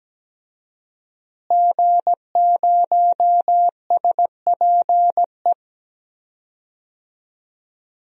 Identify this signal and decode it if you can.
Morse code, heard as a keyed tone, about 17 wpm, 700 Hz. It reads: G0SPE
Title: DXCC-100 Call Signs - 5 Characters - Repeated 3x - 17wpm